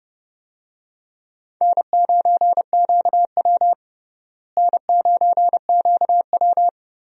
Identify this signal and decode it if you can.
Morse code, heard as a keyed tone, about 30 wpm, 700 Hz. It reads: N9QW N9QW
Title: USA Call Signs - 4 Characters - Repeated 2x - 30wpm